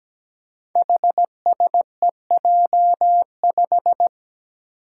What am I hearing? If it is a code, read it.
Morse code, heard as a keyed tone, about 17 wpm, 700 Hz. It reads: HSEJ5
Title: Five Challenging Letter-Number Combinations 17wpm